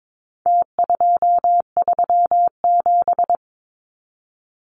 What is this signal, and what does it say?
Morse code, heard as a keyed tone, about 22 wpm, 700 Hz. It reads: T237
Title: Numbers - Contest Style - 4-Digit 22wpm